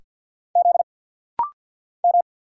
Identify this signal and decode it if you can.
Morse code, heard as a keyed tone, about 50 wpm, 690 Hz. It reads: Z M
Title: Single Letter-Number Alt - Rapid-Fire 50wpm